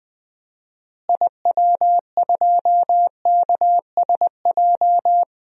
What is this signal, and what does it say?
Morse code, heard as a keyed tone, about 20 wpm, 700 Hz. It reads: IW2KSJ
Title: DXCC-100 Call Signs - 6 Characters - Repeated 2x - 20wpm